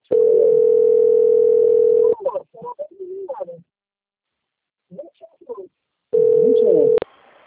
Ringback tone (North America), 440 + 480 Hz.